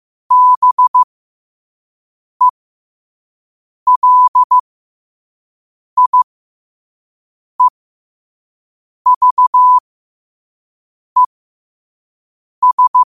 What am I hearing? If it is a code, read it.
Morse code, heard as a keyed tone, about 15 wpm, spaced as for 5 wpm, 1000 Hz. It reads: BELIEVES
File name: qotdfiles / 5wpm